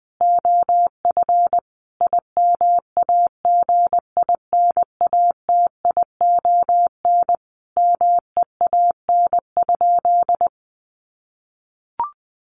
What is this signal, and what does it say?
Morse code, heard as a keyed tone, about 20 wpm, 700 Hz. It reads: OF IMAGINATION MEAN?
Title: Sentences from Top 1300 Words 20wpm